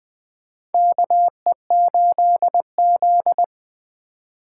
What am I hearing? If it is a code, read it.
Morse code, heard as a keyed tone, about 20 wpm, 700 Hz. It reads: KE8Z
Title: USA Call Signs - 4x Character Spacing - 20wpm